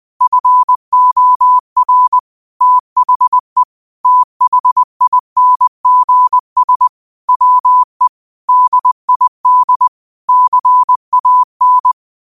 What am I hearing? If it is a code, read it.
Morse code, heard as a keyed tone, about 20 wpm, 1000 Hz. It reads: FOR THE THINGS WE DID CAN